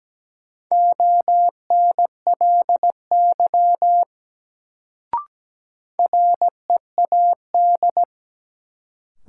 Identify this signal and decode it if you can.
Morse code, heard as a keyed tone, about 17 wpm, 700 Hz. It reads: ONLY READ
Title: Sentences from Top 200 Words - New Word Drill - 17wpm